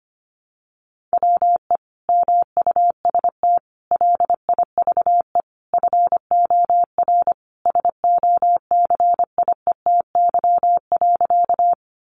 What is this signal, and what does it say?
Morse code, heard as a keyed tone, about 25 wpm, 700 Hz. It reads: WE MUST LIVE FOR SOCIETY.
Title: Sentences from Top 700 Words 25wpm